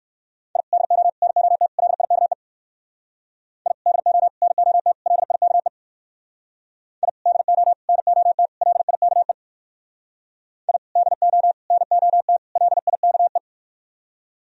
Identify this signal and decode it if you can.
Morse code, heard as a keyed tone, about 56 wpm, 700 Hz. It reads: I DO NOT LIKE I DO NOT LIKE I DO NOT LIKE I DO NOT LIKE
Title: Speed-Racing - Sets of 4 Words 45wpm